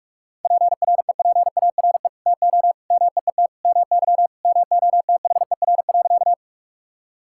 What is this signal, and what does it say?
Morse code, heard as a keyed tone, about 44 wpm, 700 Hz. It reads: PREPARE TO MEET MY MOTHER.
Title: Sentences from Top 1000 Words - Review Part 6 - 45wpm